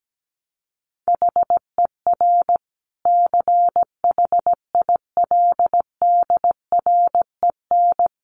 Morse code, heard as a keyed tone, about 17 wpm, 700 Hz. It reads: HER CHILDREN
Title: Sentences from Top 900 Words 17wpm